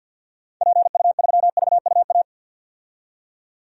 Morse code, heard as a keyed tone, about 51 wpm, 700 Hz. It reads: PU3VUA
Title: DXCC-100 Call Signs - 6 Characters - 50wpm